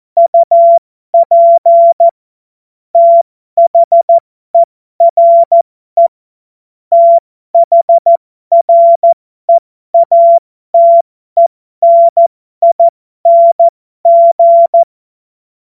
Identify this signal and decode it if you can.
Morse code, heard as a keyed tone, about 14 wpm, spaced as for 12 wpm, 680 Hz. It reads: UP THERE THREATENING
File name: Morsekurs / Witze